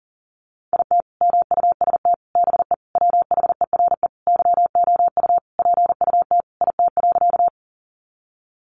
Morse code, heard as a keyed tone, about 40 wpm, 700 Hz. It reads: IT MUST BE WHERE YOU PUT IT.
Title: Sentences from Top 200 Words 40wpm